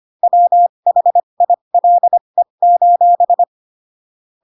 Morse code, heard as a keyed tone, about 25 wpm, spaced as for 22 wpm, 700 Hz. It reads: WHILE: